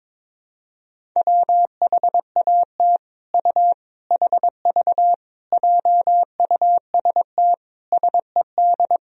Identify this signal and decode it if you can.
Morse code, heard as a keyed tone, about 22 wpm, 700 Hz. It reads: WHAT U HV JUST SED
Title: Sentences from Top 200 Words and Abbreviations 22wpm